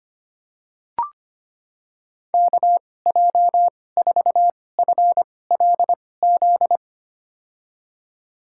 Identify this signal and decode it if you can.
Morse code, heard as a keyed tone, about 25 wpm, spaced as for 18 wpm, 700 Hz. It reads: KJ4FLZ
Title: USA Call Signs - 6 Characters - 2x Character Spacing - 25wpm